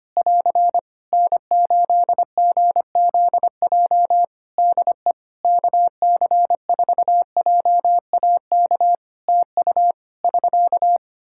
Morse code, heard as a keyed tone, about 25 wpm, 700 Hz. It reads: <AR> N8GZJ DE KC4JAK TU <SK>
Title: Practice QSO - Exchange 4 - Ending - 25wpm